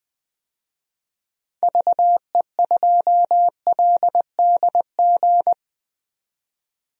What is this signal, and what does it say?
Morse code, heard as a keyed tone, about 21 wpm, 700 Hz. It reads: VE2LDG
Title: DXCC-100 Call Signs - 6 Characters - 20wpm